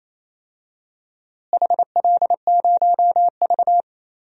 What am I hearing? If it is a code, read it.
Morse code, heard as a keyed tone, about 28 wpm, 700 Hz. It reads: HL0V
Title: DXCC-100 Call Signs - 4 Characters - Repeated 3x - 28wpm